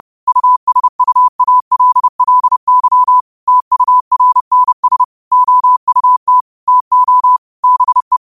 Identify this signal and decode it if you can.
Morse code, heard as a keyed tone, about 30 wpm, 1000 Hz. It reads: USUALLY TURNS OUT TO BE
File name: qotdfiles / 30wpm